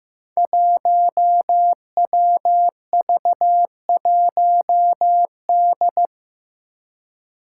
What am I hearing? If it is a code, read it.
Morse code, heard as a keyed tone, about 15 wpm, 700 Hz. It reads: 1WV1D
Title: Five Challenging Letter-Number Combinations 15wpm